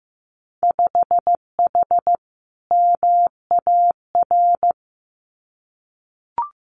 Morse code, heard as a keyed tone, about 15 wpm, 700 Hz. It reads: 5H MAR
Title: Winter Field Day Exchanges Encore 15wpm